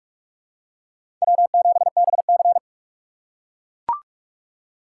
Morse code, heard as a keyed tone, about 44 wpm, 690 Hz. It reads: W7BC
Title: USA Call Signs - 4 Characters - 2x Character Spacing - 45wpm